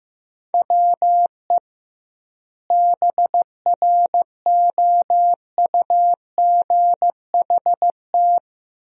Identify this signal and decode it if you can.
Morse code, heard as a keyed tone, about 15 wpm, 700 Hz. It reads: WE BROUGHT
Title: 2x Word-Spacing - Sentences from Top 500 Words - 15wpm